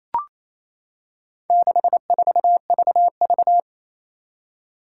Morse code, heard as a keyed tone, about 27 wpm, 700 Hz. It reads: 64VV
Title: Four Challenging Letter-Number Combinations 28wpm